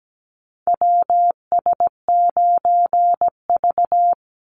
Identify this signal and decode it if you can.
Morse code, heard as a keyed tone, about 17 wpm, 700 Hz. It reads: WS9V
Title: Call Signs Repeated 3x - 17wpm